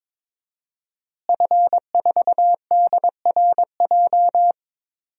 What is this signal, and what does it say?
Morse code, heard as a keyed tone, about 21 wpm, 700 Hz. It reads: F4DRJ
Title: DXCC-100 Call Signs - 5 Characters - Repeated 3x - 22wpm